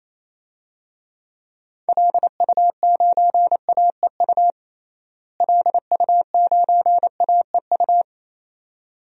Morse code, heard as a keyed tone, about 28 wpm, 700 Hz. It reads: LU9AEU LU9AEU
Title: DXCC-100 Call Signs - 6 Characters - Repeated 2x - 28wpm